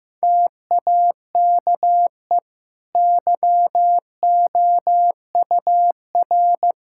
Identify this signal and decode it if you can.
Morse code, heard as a keyed tone, about 16 wpm, 700 Hz. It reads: TAKE YOUR
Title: Sentences from Top 1000 Words - Review Part 6 - 15wpm